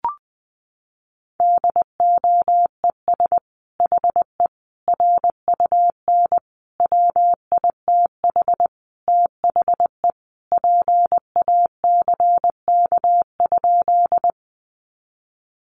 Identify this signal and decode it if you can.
Morse code, heard as a keyed tone, about 20 wpm, 700 Hz. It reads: DOES HE RUN WITH THE PACK?
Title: Sentences from Top 1300 Words 20wpm